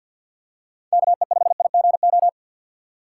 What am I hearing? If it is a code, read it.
Morse code, heard as a keyed tone, about 50 wpm, 700 Hz. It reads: KE5IGO
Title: USA Call Signs - 6 Characters - 6x Character Spacing - 50wpm